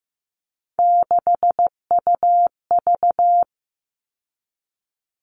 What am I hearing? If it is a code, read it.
Morse code, heard as a keyed tone, about 15 wpm, 700 Hz. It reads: 6UV
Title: All Cut Numbers - 3-Digit 15wpm